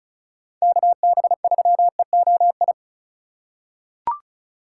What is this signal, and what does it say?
Morse code, heard as a keyed tone, about 34 wpm, 700 Hz. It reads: KB3EOI
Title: USA Call Signs - 6 Characters - 2x Character Spacing - 35wpm